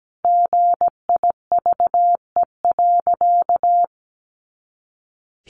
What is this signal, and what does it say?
Morse code, heard as a keyed tone, about 17 wpm, 700 Hz. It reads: GIVE.
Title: Sentences from Top 200 Words 17wpm